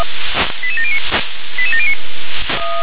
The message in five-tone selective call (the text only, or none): 88709 88709